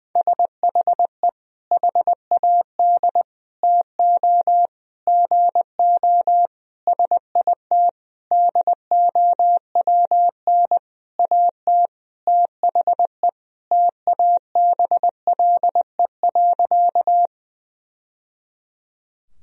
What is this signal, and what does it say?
Morse code, heard as a keyed tone, about 20 wpm, 700 Hz. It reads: SHE HAD TO GO SIT DOWN AT THE TABLE.
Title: Sentences from Top 1000 Words - Review Part 2 - 20wpm